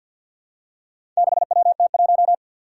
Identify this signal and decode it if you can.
Morse code, heard as a keyed tone, about 48 wpm, 700 Hz. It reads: 6WT1
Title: Four Challenging Letter-Number Combinations 50wpm